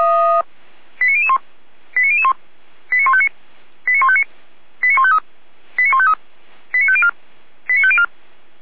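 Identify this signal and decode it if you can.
Five-tone selective call (ZVEI), heard as quick runs of five tones, tones of about 70 ms. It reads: 89001 89001 89159 89159 89153 89153 89504 89504
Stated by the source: Five-tone ZVEI selective call received by SDR.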